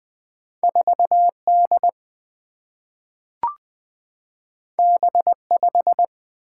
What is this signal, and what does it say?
Morse code, heard as a keyed tone, about 20 wpm, 700 Hz. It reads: 4D B5
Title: Single-Letter Single-Number 20wpm